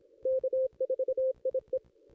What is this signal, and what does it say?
Morse code, heard as a keyed tone, about 26 wpm, 520 Hz. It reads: K4IE
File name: contests / ss-cw-k5tr-op-n5rz -2200